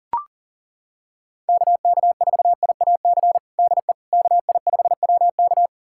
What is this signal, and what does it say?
Morse code, heard as a keyed tone, about 40 wpm, 700 Hz. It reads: KK4IAC DE KI5WK
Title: USA Callsign DE Callsign 40wpm